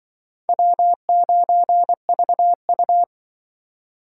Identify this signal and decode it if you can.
Morse code, heard as a keyed tone, about 25 wpm, 700 Hz. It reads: W9VU
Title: Speed Racing - USA 1x2 Call Signs 20wpm